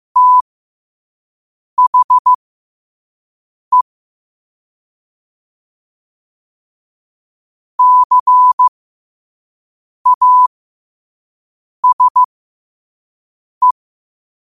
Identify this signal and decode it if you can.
Morse code, heard as a keyed tone, about 15 wpm, spaced as for 5 wpm, 1000 Hz. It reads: THE CASE
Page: Quotes for Wed, 13 Aug 2025 in Morse Code at 5 words per minute.